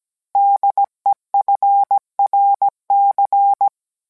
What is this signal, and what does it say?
Morse code, heard as a keyed tone, about 17 wpm, 800 Hz. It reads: DEFRC